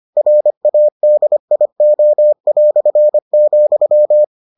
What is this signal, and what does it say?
Morse code, heard as a keyed tone, about 25 wpm, 600 Hz. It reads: RADIO",